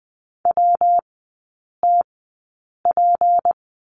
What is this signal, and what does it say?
Morse code, heard as a keyed tone, about 20 wpm, spaced as for 8 wpm, 700 Hz. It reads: WTP
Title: 198 Lesson 32e - Letter k - Three Characters 20wpm